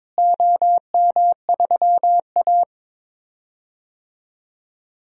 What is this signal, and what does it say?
Morse code, heard as a keyed tone, about 21 wpm, 700 Hz. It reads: OM3A
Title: DXCC-100 Call Signs - 4 Characters - Repeated 3x - 22wpm